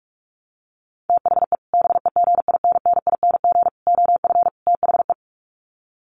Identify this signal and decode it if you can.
Morse code, heard as a keyed tone, about 44 wpm, 700 Hz. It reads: THE BEGINNING OF THE
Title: Sets of 4 Words 45wpm